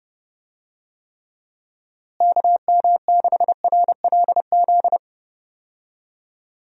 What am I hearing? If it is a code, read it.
Morse code, heard as a keyed tone, about 32 wpm, 700 Hz. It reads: KM6RLZ
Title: USA Call Signs - 6 Characters - Repeated 3x - 30wpm